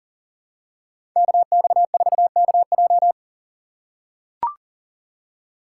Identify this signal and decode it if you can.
Morse code, heard as a keyed tone, about 40 wpm, 700 Hz. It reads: KX4KJ
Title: USA Call Signs - 5 Characters - 4x Character Spacing - 40wpm